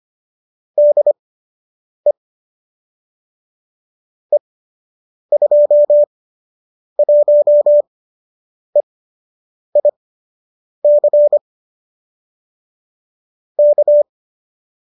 Morse code, heard as a keyed tone, about 25 wpm, spaced as for 8 wpm, 570 Hz. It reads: DE E21EIC K